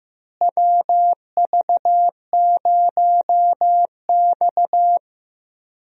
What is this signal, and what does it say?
Morse code, heard as a keyed tone, about 15 wpm, 700 Hz. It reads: WV0X
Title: USA Call Signs - 4 Characters - Repeated 2x - 15wpm